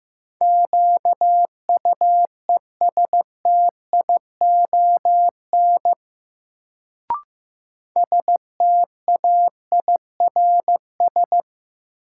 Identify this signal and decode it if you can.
Morse code, heard as a keyed tone, about 15 wpm, 700 Hz. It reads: QUESTION STAIRS
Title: Sets of 5 Words Encore - New Word Drill - 15wpm